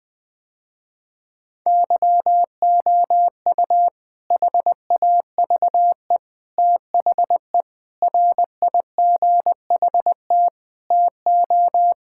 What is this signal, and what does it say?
Morse code, heard as a keyed tone, about 20 wpm, 700 Hz. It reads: YOU HAVE THE RIGHT TO
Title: Sentences with 4-Word Phrases - Spoken Beginning - 20wpm